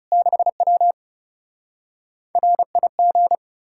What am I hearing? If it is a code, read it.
Morse code, heard as a keyed tone, about 40 wpm, 700 Hz. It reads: 6W RIG